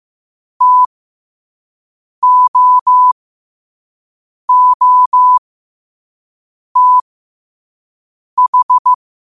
Morse code, heard as a keyed tone, about 15 wpm, spaced as for 5 wpm, 1000 Hz. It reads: TOOTH
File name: 5wpm